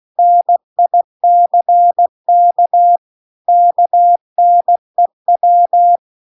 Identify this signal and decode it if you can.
Morse code, heard as a keyed tone, about 16 wpm, 700 Hz. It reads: NICK KNEW
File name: CW-The-Big-Twoheaded-River-2-16WPM